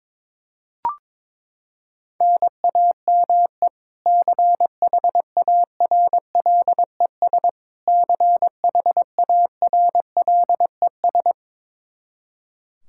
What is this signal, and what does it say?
Morse code, heard as a keyed tone, about 21 wpm, 700 Hz. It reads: NAME CHARLES CHARLES
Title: Practice QSO - Exchange 1 - Beginning - 22wpm